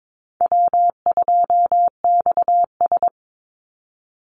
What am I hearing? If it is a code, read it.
Morse code, heard as a keyed tone, about 22 wpm, 700 Hz. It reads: W2XS
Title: Easy Call Signs Repeated 2x - 22wpm